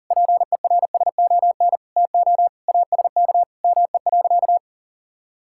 Morse code, heard as a keyed tone, about 40 wpm, 700 Hz. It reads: PERSON TO ASK ME.
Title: Sentences from Top 1000 Words - Review Part 5 - 40wpm